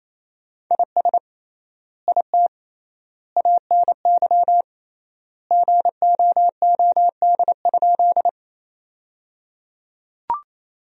Morse code, heard as a keyed tone, about 28 wpm, 700 Hz. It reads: IS IT ANY GOOD?
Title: 3x Word-Spacing - Sentences from Top 200 Words - 28wpm